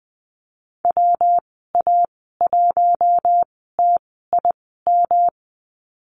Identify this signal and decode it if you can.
Morse code, heard as a keyed tone, about 20 wpm, spaced as for 15 wpm, 700 Hz. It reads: WA1TIM
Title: 122 Lesson 20g - Number 3 - Call Signs 20wpm